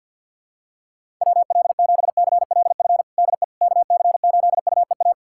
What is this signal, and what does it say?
Morse code, heard as a keyed tone, about 53 wpm, 700 Hz. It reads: WL7CLF DE KC8UEA
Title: USA Callsign DE Callsign - 6 Characters - 50wpm